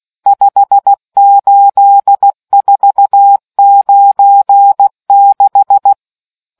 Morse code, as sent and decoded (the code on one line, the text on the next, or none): ..... ---.. ....- ----. -....
58496